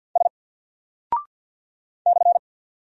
Morse code, heard as a keyed tone, about 48 wpm, 700 Hz. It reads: S /
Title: Single Character (Letter:Number:Punctuation:Prosign) 50wpm